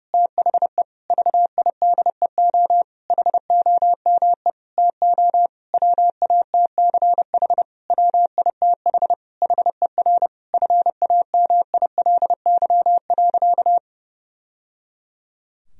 Morse code, heard as a keyed tone, about 30 wpm, 700 Hz. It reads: THE VIDEO HOME TO WATCH WITH HER FAMILY.
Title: Sentences from Top 1000 Words - Review Part 4 - 30wpm